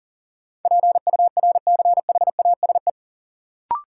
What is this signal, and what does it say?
Morse code, heard as a keyed tone, about 40 wpm, 700 Hz. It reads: PURCHASE
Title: Sentences from Top 800 Words - New Word Drill - 40wpm